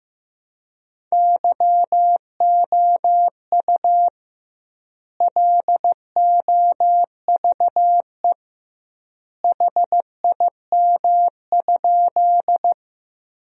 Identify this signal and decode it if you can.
Morse code, heard as a keyed tone, about 16 wpm, 700 Hz. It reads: YOU LOVE HIM?
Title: 2x Word-Spacing - Sentences from Top 400 Words - 15wpm